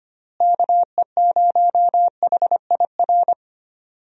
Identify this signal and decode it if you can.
Morse code, heard as a keyed tone, about 25 wpm, 700 Hz. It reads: KE0HIR
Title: USA Call Signs - 6 Characters - Repeated 3x - 25wpm